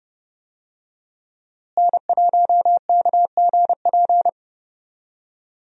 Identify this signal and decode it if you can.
Morse code, heard as a keyed tone, about 29 wpm, 700 Hz. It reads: N1KGP
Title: USA Call Signs - 5 Characters - Repeated 3x - 30wpm